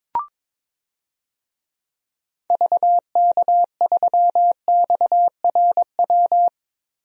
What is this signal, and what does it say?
Morse code, heard as a keyed tone, about 22 wpm, 700 Hz. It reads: VK3XRW
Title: DXCC-100 Call Signs - 6 Characters - Repeated 2x - 22wpm